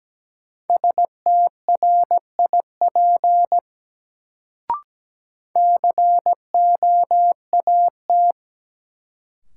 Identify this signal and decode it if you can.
Morse code, heard as a keyed tone, about 17 wpm, 700 Hz. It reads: STRIP COAT
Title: Sentences from Top 1600 Words - New Word Drill Alt - 17wpm